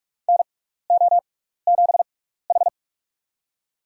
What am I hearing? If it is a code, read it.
Morse code, heard as a keyed tone, about 44 wpm, spaced as for 15 wpm, 700 Hz. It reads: NO7H
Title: USA Call Signs - 6x Character Spacing - 45wpm